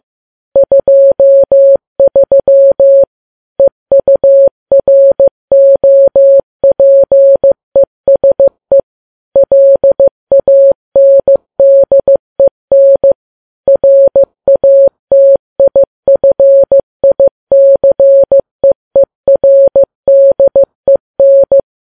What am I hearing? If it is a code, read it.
Morse code, heard as a keyed tone, about 15 wpm, 550 Hz. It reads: 23 EUROPESE LANDEN RATIFICEERDEN
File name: Files Web tls / Morse on9fv / morse15kps